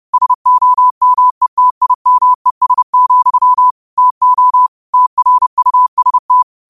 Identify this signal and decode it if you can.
Morse code, heard as a keyed tone, about 30 wpm, 1000 Hz. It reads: SOMETIMES, TO TRUST
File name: 30wpm